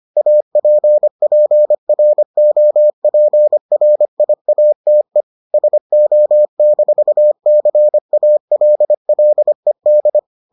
Morse code, heard as a keyed tone, about 25 wpm, 600 Hz. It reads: APPROPRIATE SO-CALLED